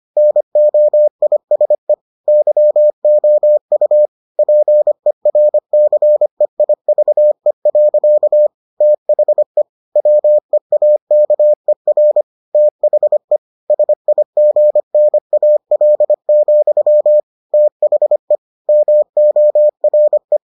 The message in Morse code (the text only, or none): NOISE YOU PERCEIVE. THE WEAKER THE SIGNAL, THE MORE